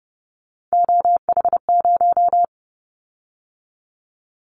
Morse code, heard as a keyed tone, about 30 wpm, 700 Hz. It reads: OH0
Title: Callsign Prefixes for Top 100 DXCC - 30wpm